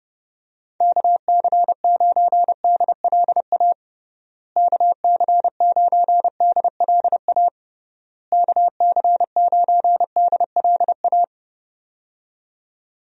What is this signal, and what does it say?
Morse code, heard as a keyed tone, about 30 wpm, 700 Hz. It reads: KC9DLA KC9DLA KC9DLA
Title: USA Call Signs - 6 Characters - Repeated 3x - 30wpm